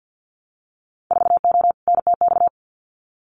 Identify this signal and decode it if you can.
Morse code, heard as a keyed tone, about 50 wpm, 700 Hz. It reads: 4O NTX
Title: Winter Field Day Exchanges 50wpm